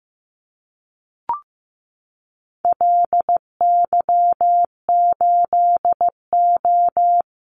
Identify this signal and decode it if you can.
Morse code, heard as a keyed tone, about 15 wpm, 700 Hz. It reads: LY8O
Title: Call Signs Repeated 2x - 15wpm